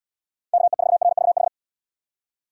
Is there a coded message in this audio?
Yes: Morse code from 0.5 to 1.5 s, heard as a keyed tone, about 77 wpm, 700 Hz. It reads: KE4RVH